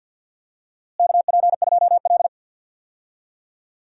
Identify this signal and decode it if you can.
Morse code, heard as a keyed tone, about 51 wpm, 700 Hz. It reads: KP2L
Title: DXCC-100 Call Signs - 4 Characters - Repeated 2x - 50wpm